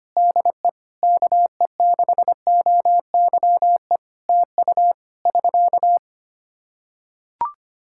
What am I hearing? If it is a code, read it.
Morse code, heard as a keyed tone, about 25 wpm, 700 Hz. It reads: DE KE6OYE TU <SK>
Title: Practice QSO - Exchange 4 - Ending - 25wpm